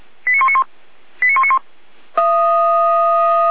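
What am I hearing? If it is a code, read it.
Five-tone selective call (ZVEI), heard as quick runs of five tones, tones of about 70 ms. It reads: 89191 89191
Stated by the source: ZVEI five-tone selective call received with an SDR.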